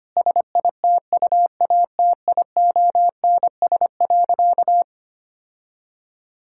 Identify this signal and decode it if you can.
Morse code, heard as a keyed tone, about 25 wpm, 700 Hz. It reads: SITUATIONS.
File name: practice / 25wpm / Difficult Sentences with Hints - 25wpm